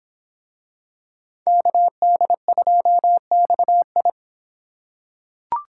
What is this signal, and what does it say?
Morse code, heard as a keyed tone, about 26 wpm, 700 Hz. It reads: KD2XI